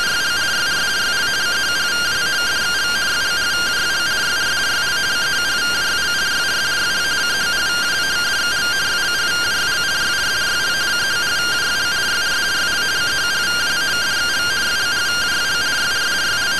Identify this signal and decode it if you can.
Two-tone teleprinter signal, about 170 Hz shift.
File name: findings / SAGITTAL 43mm 4mm.gen-15 rtty rx